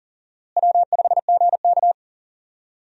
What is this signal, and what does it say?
Morse code, heard as a keyed tone, about 40 wpm, 700 Hz. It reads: W5GK
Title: USA Call Signs - 4 Characters - Repeated 3x - 40wpm